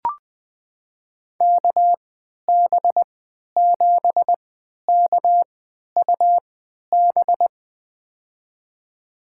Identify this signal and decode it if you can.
Morse code, heard as a keyed tone, about 20 wpm, spaced as for 11 wpm, 700 Hz. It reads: KB7KUB
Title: USA Call Signs - 6 Characters - 3x Character Spacing - 20wpm